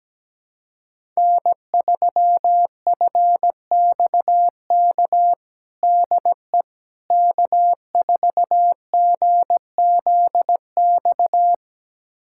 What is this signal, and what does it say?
Morse code, heard as a keyed tone, about 17 wpm, 700 Hz. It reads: N3FXK DE K4GZX
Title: USA Callsign DE Callsign - 5 Characters - 17wpm